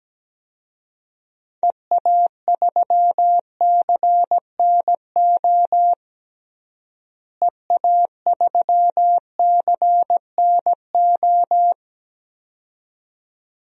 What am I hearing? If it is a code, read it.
Morse code, heard as a keyed tone, about 17 wpm, 700 Hz. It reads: EA3CNO EA3CNO
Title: DXCC-100 Call Signs - 6 Characters - Repeated 2x - 17wpm